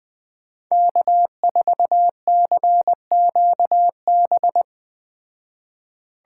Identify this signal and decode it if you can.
Morse code, heard as a keyed tone, about 21 wpm, 700 Hz. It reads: K4CQB